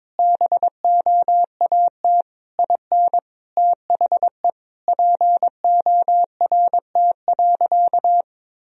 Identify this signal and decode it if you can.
Morse code, heard as a keyed tone, about 22 wpm, 700 Hz. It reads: BOAT IN THE PORT.
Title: Sentences from Top 1000 Words - Review Part 1 - 22wpm